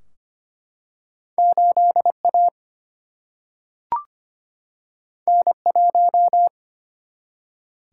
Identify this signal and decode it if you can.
Morse code, heard as a keyed tone, about 25 wpm, 700 Hz. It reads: 8A N1
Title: Single-Letter Single-Number 25wpm